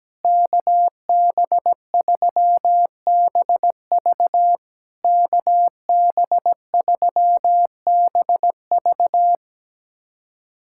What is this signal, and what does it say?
Morse code, heard as a keyed tone, about 17 wpm, 700 Hz. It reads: KB3BV KB3BV
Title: SOTA QSO Practice 17wpm